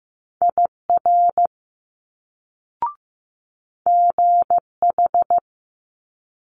Morse code, heard as a keyed tone, about 15 wpm, 700 Hz. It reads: IR GH
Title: Multi-letter Phonemes Quick Fire 15wpm